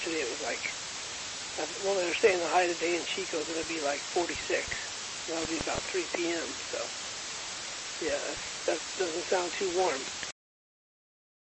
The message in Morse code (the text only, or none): none